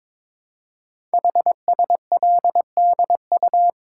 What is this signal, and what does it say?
Morse code, heard as a keyed tone, about 22 wpm, 700 Hz. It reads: HSLDU
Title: Code Groups - 5 Letter:Numbers - 22wpm